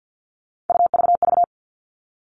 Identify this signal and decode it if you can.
Morse code, heard as a keyed tone, about 50 wpm, 700 Hz. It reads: UVV
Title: Three Challenging Letter-Number Combinations 50wpm